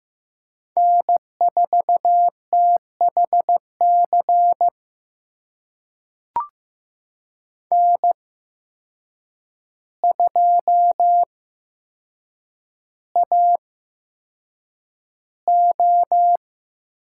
Morse code, heard as a keyed tone, about 15 wpm, spaced as for 7 wpm, 700 Hz. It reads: N4THC N2AO
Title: USA Call Signs - 5 Characters - 8x Character Spacing - 15wpm